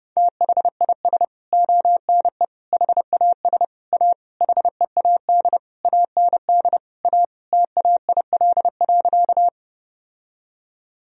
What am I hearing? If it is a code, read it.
Morse code, heard as a keyed tone, about 32 wpm, 700 Hz. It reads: THIS ONE HAS A HEAD AND A TAIL.
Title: Sentences from Top 1000 Words - Review Part 3 - 30wpm